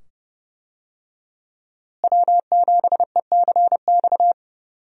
Morse code, heard as a keyed tone, about 30 wpm, 700 Hz. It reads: W7ECX
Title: USA Call Signs - 5 Characters - Repeated 2x - 30wpm